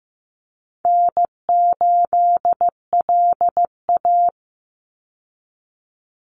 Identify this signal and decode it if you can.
Morse code, heard as a keyed tone, about 15 wpm, 700 Hz. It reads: N8LA
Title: Easy Call Signs - 15wpm